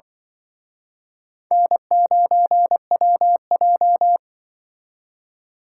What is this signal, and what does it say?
Morse code, heard as a keyed tone, about 24 wpm, 700 Hz. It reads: N9WJ
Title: Speed Racing - USA 1x2 Call Signs 20wpm